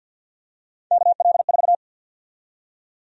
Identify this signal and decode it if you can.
Morse code, heard as a keyed tone, about 53 wpm, 700 Hz. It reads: KL4
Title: USA Call Sign Prefixes - 50wpm